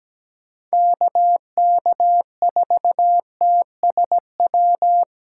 Morse code, heard as a keyed tone, about 17 wpm, 700 Hz. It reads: KK4TSW
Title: USA Callsign DE Callsign - 6 Characters - 17wpm